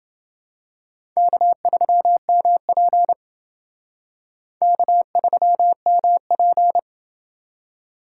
Morse code, heard as a keyed tone, about 29 wpm, 700 Hz. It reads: K3MP K3MP